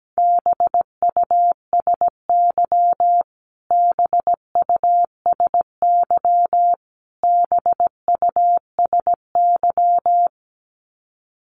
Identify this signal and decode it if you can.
Morse code, heard as a keyed tone, about 17 wpm, 700 Hz. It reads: BUSY BUSY BUSY
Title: Top 500 Words Repeated 17wpm